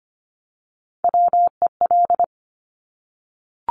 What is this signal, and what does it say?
Morse code, heard as a keyed tone, about 25 wpm, 700 Hz. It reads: WEL
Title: Top Three Letter Combinations in Words 25wpm